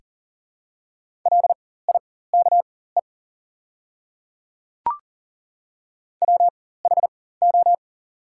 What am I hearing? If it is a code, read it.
Morse code, heard as a keyed tone, about 40 wpm, spaced as for 19 wpm, 700 Hz. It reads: LIKE WHO
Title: 4x Character Spacing - Top 100 Words - 40wpm